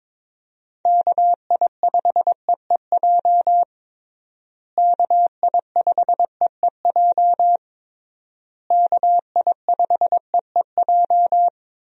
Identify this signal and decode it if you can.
Morse code, heard as a keyed tone, about 22 wpm, 700 Hz. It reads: KI5EEJ KI5EEJ KI5EEJ
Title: USA Call Signs - 6 Characters - Repeated 3x - 22wpm